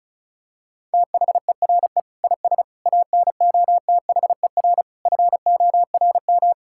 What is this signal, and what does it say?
Morse code, heard as a keyed tone, about 36 wpm, 700 Hz. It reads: THERE IS ANOTHER FORM
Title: Sentences from Top 1000 Words - Review Part 6 - 35wpm